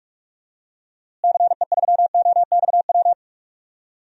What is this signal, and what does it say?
Morse code, heard as a keyed tone, about 46 wpm, 700 Hz. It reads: CE3OXW
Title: DXCC-100 Call Signs - 6 Characters - Repeated 3x - 45wpm